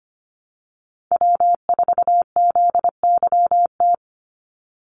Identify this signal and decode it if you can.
Morse code, heard as a keyed tone, about 25 wpm, 700 Hz. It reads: W4ZYT
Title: Call Signs - 25wpm